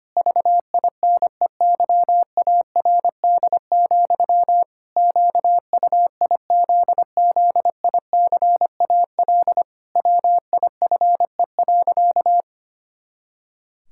Morse code, heard as a keyed tone, about 25 wpm, 700 Hz. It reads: VINEYARD, QUIZZICAL WIFE.
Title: Pangram Challenge v2 - 25wpm